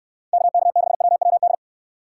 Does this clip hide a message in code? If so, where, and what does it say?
Yes: Morse code from 0.3 to 1.5 s, heard as a keyed tone, about 69 wpm, 700 Hz. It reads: KK6WWD